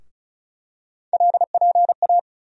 Morse code, heard as a keyed tone, about 36 wpm, 700 Hz. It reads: LPA
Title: USA Call Sign Postfixes - 3 Letters 35wpm